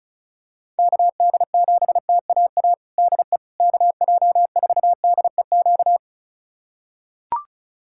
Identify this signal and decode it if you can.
Morse code, heard as a keyed tone, about 34 wpm, 700 Hz. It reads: KD7TAA DE KJ4DEQ
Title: USA Callsign DE Callsign 35wpm